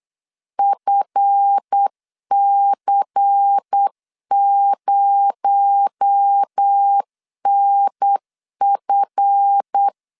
Morse code, heard as a keyed tone, about 8 wpm, 800 Hz. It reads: FC0NF
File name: LEZIONI / 7 GF9Z / 7 GF 8wpm 3108